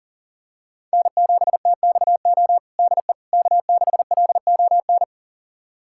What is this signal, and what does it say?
Morse code, heard as a keyed tone, about 40 wpm, 700 Hz. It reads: N7TXO DE K6LON
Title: USA Callsign DE Callsign - 5 Characters - 40wpm